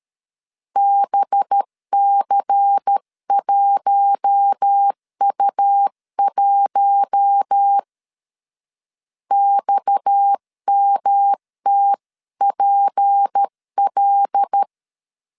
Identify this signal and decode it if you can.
Morse code, heard as a keyed tone, about 13 wpm, 800 Hz. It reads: BC1U1 XMTPL
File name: CW / DET / 1 6 1500C S C 12wpm 3533